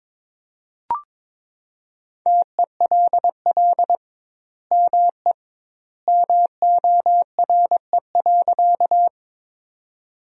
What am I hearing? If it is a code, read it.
Morse code, heard as a keyed tone, about 22 wpm, 700 Hz. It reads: TELL ME MORE.
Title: 2x Word-Spacing - Sentences from Top 300 Words - 22wpm